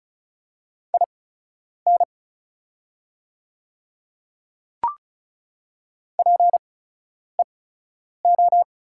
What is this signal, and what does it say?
Morse code, heard as a keyed tone, about 34 wpm, spaced as for 10 wpm, 700 Hz. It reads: IN PEO
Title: 8x Character Spacing - Top 300 Words - 35wpm